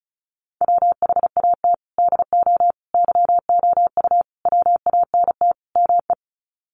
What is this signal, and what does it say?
Morse code, heard as a keyed tone, about 34 wpm, 700 Hz. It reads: WHAT DO YOU WANT ME
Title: Sets of 5 Words Encore2 35wpm